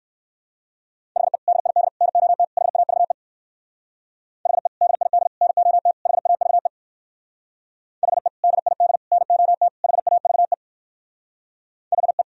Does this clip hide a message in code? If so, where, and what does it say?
Yes: Morse code from 1.2 to 12.2 s, heard as a keyed tone, about 58 wpm, 700 Hz. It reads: HE DID NOT HAVE HE DID NOT HAVE HE DID NOT HAVE HE